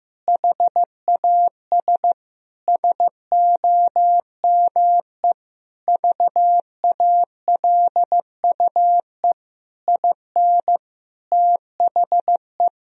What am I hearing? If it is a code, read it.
Morse code, heard as a keyed tone, about 15 wpm, 700 Hz. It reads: HAS SOME VALUE IN THE
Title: Sentences from Top 1000 Words - Review Part 3 - 15wpm